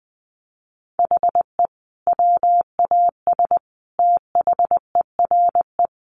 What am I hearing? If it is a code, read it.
Morse code, heard as a keyed tone, about 20 wpm, 700 Hz. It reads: HE WAS THERE
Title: Sentences from Top 200 Words 20wpm